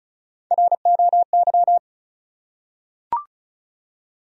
Morse code, heard as a keyed tone, about 36 wpm, 700 Hz. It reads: ROY
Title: USA Common Names - 35wpm